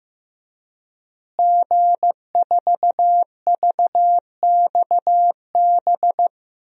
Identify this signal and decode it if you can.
Morse code, heard as a keyed tone, about 15 wpm, 700 Hz. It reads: G4VXB
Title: DXCC-100 Call Signs - 5 Characters - Repeated 2x - 15wpm